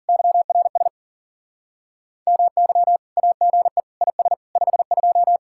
Morse code, heard as a keyed tone, about 55 wpm, 700 Hz. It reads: YRS MY AGE IS 52